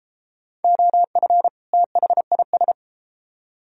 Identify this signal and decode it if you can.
Morse code, heard as a keyed tone, about 32 wpm, 700 Hz. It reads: OF THIS